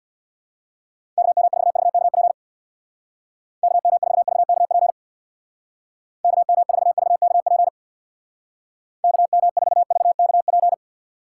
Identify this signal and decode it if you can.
Morse code, heard as a keyed tone, about 65 wpm, 700 Hz. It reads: KM4VKP KM4VKP KM4VKP KM4VKP